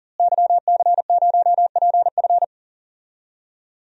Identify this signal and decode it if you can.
Morse code, heard as a keyed tone, about 39 wpm, 700 Hz. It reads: YC0PF
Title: DXCC-100 Call Signs - 5 Characters - Repeated 2x - 40wpm